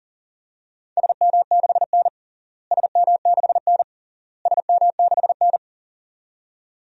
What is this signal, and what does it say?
Morse code, heard as a keyed tone, about 40 wpm, 700 Hz. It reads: SM6N SM6N SM6N
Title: DXCC-100 Call Signs - 4 Characters - Repeated 3x - 40wpm